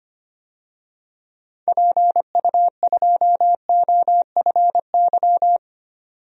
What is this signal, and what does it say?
Morse code, heard as a keyed tone, about 25 wpm, 700 Hz. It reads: PU2OFY
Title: DXCC-100 Call Signs - 6 Characters - 25wpm